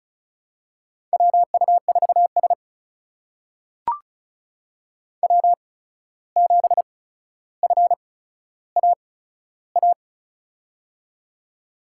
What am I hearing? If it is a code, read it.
Morse code, heard as a keyed tone, about 36 wpm, spaced as for 14 wpm, 700 Hz. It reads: WU4S W7FAA
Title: USA Call Signs - 8x Character Spacing - 35wpm